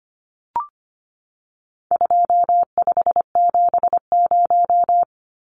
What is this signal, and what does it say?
Morse code, heard as a keyed tone, about 25 wpm, 700 Hz. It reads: 2570
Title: Numbers - 4-Digit 25wpm